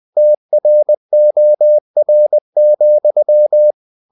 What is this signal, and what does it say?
Morse code, heard as a keyed tone, about 20 wpm, 600 Hz. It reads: TROR,
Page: Ferie 20 wpm